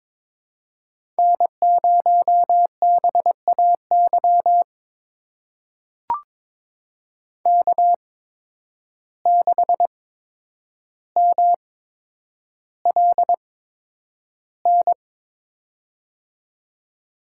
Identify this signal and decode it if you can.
Morse code, heard as a keyed tone, about 22 wpm, spaced as for 9 wpm, 700 Hz. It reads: N0BAY K6MLN
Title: USA Call Signs - 5 Characters - 8x Character Spacing - 22wpm